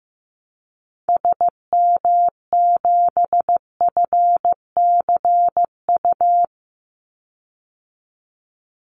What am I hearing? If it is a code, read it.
Morse code, heard as a keyed tone, about 15 wpm, 700 Hz. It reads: SM7FCU
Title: Call Signs Repeated 2x - 15wpm